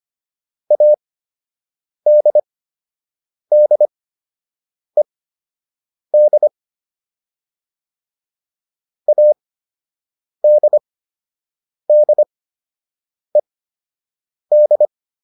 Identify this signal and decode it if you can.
Morse code, heard as a keyed tone, about 25 wpm, spaced as for 7 wpm, 600 Hz. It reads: ADDED ADDED